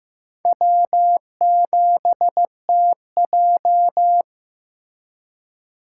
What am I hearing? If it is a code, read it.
Morse code, heard as a keyed tone, about 15 wpm, 700 Hz. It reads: W7TJ
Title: USA Call Signs - 2x Character Spacing - 15wpm